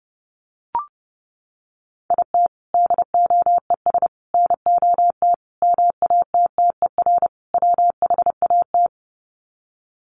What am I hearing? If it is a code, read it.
Morse code, heard as a keyed tone, about 30 wpm, 700 Hz. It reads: IT DOES NOT MATTER WHAT
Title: Sets of 5 Words Encore 30wpm